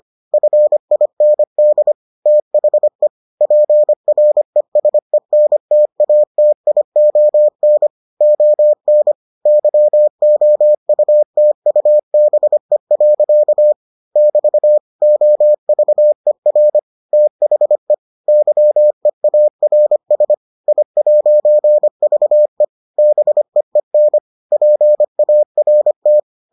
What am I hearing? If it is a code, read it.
Morse code, heard as a keyed tone, about 25 wpm, 600 Hz. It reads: FIND THE PRESENTATION ON YOUTUBE. <BT> OVER THE YEARS I'VE BEEN PART